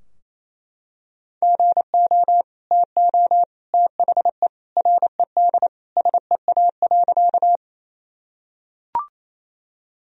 Morse code, heard as a keyed tone, about 28 wpm, 700 Hz. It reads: GO TO THE RED SEA.
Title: Sentences from Top 1000 Words - Review Part 2 - 28wpm